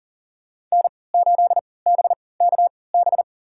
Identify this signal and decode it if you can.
Morse code, heard as a keyed tone, about 40 wpm, spaced as for 23 wpm, 700 Hz. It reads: N8BKB
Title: USA Call Signs - 5 Characters - 3x Character Spacing - 40wpm